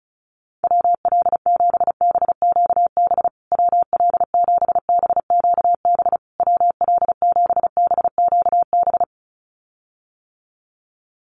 Morse code, heard as a keyed tone, about 34 wpm, 700 Hz. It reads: WL7BQB WL7BQB WL7BQB
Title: USA Call Signs Repeated 3x 35wpm